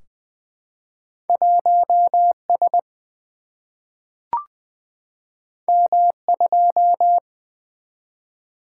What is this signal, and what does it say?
Morse code, heard as a keyed tone, about 20 wpm, 700 Hz. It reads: 1 S M 2
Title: Single-Letter Single-Number 20wpm